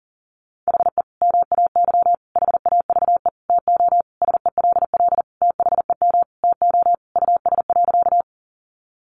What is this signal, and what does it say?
Morse code, heard as a keyed tone, about 40 wpm, 700 Hz. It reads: HE MAY HAVE TO SELL THEM TO US.
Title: Sentences from Top 1100 Words 40wpm